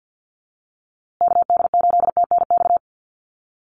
Morse code, heard as a keyed tone, about 50 wpm, 700 Hz. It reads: KD8TNX
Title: USA Call Signs Repeated 3x 50wpm